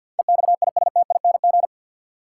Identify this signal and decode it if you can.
Morse code, heard as a keyed tone, about 52 wpm, 700 Hz. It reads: EXISTING
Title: Sentences from Top 1500 Words - New Word Drill - 50wpm